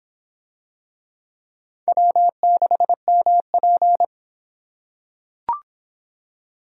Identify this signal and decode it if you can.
Morse code, heard as a keyed tone, about 23 wpm, 700 Hz. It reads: W6MP